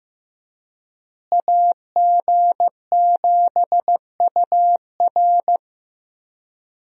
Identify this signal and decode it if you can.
Morse code, heard as a keyed tone, about 15 wpm, 700 Hz. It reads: AG7UR
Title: USA Call Signs - 5 Characters - Repeated 2x - 15wpm